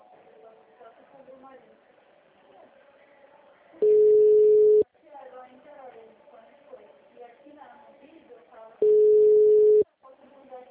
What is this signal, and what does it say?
Ringback tone (Europe, 425 Hz), with speech.